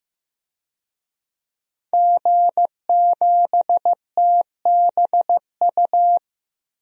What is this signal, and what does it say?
Morse code, heard as a keyed tone, about 15 wpm, 700 Hz. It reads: G7TBU
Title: DXCC-100 Call Signs - 5 Characters - Repeated 3x - 15wpm